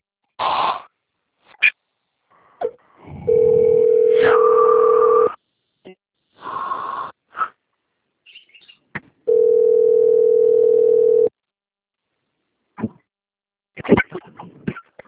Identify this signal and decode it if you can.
Ringback tone (North America), 440 + 480 Hz.
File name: rg-1600-unknown- - - .14753